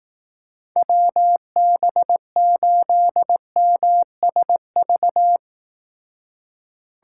Morse code, heard as a keyed tone, about 18 wpm, 700 Hz. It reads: WB8MSV
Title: Speed Racing - USA 2x3 Call Signs 15wpm